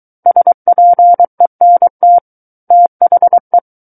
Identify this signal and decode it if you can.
Morse code, heard as a keyed tone, about 23 wpm, 700 Hz. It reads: SPENT THE
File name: Book One / 14-Chapter 14 - 23 WPM